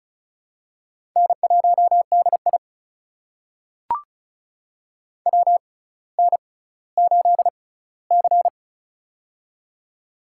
Morse code, heard as a keyed tone, about 36 wpm, spaced as for 18 wpm, 700 Hz. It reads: N1DI WN8C
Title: USA Call Signs - 6x Character Spacing - 35wpm